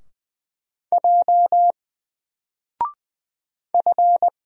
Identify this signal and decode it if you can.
Morse code, heard as a keyed tone, about 20 wpm, 700 Hz. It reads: J F
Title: Single Letter-Number Alt 20wpm